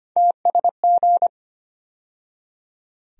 Morse code, heard as a keyed tone, about 25 wpm, 700 Hz. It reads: TSG
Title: USA Call Sign Postfixes - 3 Letters 25wpm